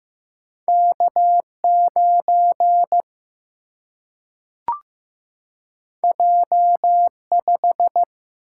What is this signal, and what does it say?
Morse code, heard as a keyed tone, about 15 wpm, 700 Hz. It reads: K 9 J 5
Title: Single-Letter Single-Number 15wpm